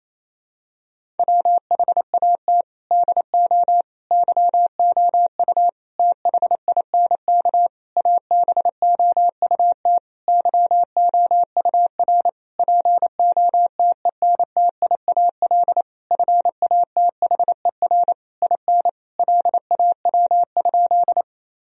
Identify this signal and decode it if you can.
Morse code, heard as a keyed tone, about 28 wpm, 700 Hz. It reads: WHAT DO YOU THINK ABOUT YOUR POTENTIAL FATHER IN LAW?
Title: Sentences from Top 1000 Words - Review Part 4 - 28wpm